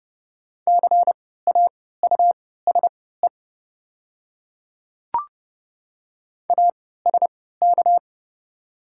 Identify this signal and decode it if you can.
Morse code, heard as a keyed tone, about 30 wpm, spaced as for 17 wpm, 700 Hz. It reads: CAUSE ASK
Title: 3x Character Spacing - Top 500 Words - 30wpm